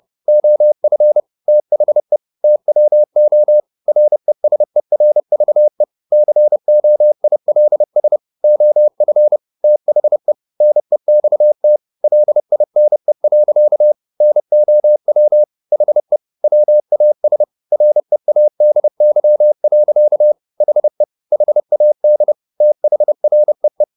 Morse code, heard as a keyed tone, about 30 wpm, 600 Hz. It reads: OF THE TWO RESERVE COILS OF THE NEXT LINE. NOW HE WAS READY. HE HAD THREE
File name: the old man and the sea / the old man and the sea 1 30wpm